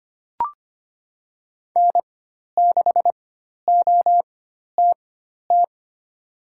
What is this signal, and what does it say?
Morse code, heard as a keyed tone, about 25 wpm, spaced as for 12 wpm, 700 Hz. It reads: N6OTT
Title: USA Call Signs - 4x Character Spacing - 25wpm